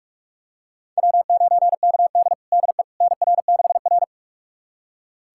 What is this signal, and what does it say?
Morse code, heard as a keyed tone, about 46 wpm, 700 Hz. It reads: W9KD DE NR6R
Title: USA Callsign DE Callsign - 4 Characters - 45wpm